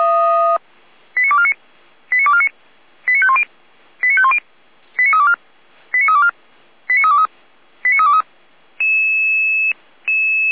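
Five-tone selective call (ZVEI), heard as quick runs of five tones, tones of about 70 ms. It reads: 89249 89249 89510 89510 89324 89324 89323 89323
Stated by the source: ZVEI five-tone selective call received by SDR, logged by the fire brigade server feuerwehr-orlamuende.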